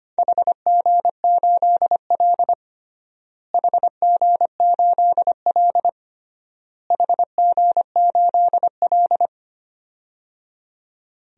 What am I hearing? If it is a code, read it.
Morse code, heard as a keyed tone, about 25 wpm, 700 Hz. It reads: HG8L HG8L HG8L
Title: DXCC-100 Call Signs - 4 Characters - Repeated 3x - 25wpm